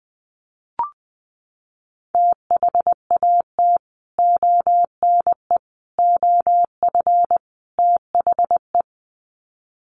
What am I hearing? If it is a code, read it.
Morse code, heard as a keyed tone, about 20 wpm, 700 Hz. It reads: THAT ONE OF THE
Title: Sets of 4 Words 20wpm